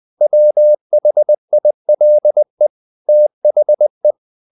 Morse code, heard as a keyed tone, about 20 wpm, 600 Hz. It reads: WHILE THE
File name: the old man and the sea part 2 20wpm